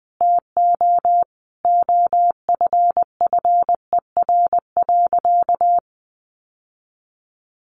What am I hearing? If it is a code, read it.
Morse code, heard as a keyed tone, about 20 wpm, 700 Hz. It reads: TO OFFER.
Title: Sentences from Top 1100 Words 20wpm